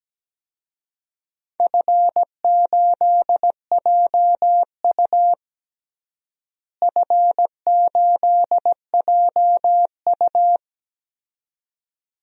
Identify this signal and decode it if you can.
Morse code, heard as a keyed tone, about 17 wpm, 700 Hz. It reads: F8JU F8JU
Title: DXCC-100 Call Signs - 4 Characters - Repeated 2x - 17wpm